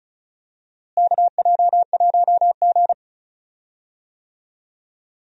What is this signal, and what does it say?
Morse code, heard as a keyed tone, about 36 wpm, 700 Hz. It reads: KJ1G
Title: USA Call Signs - 4 Characters - Repeated 3x - 35wpm